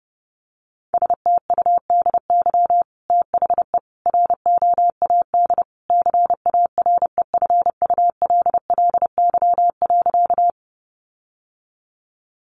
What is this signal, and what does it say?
Morse code, heard as a keyed tone, about 30 wpm, 700 Hz. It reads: STUDY THE ROAD CAREFULLY.
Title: Sentences from Top 1300 Words 30wpm